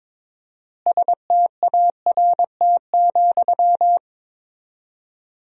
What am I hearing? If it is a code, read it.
Morse code, heard as a keyed tone, about 22 wpm, 700 Hz. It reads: START,
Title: 4x Word-Spacing - Sentences from Top 300 Words - 22wpm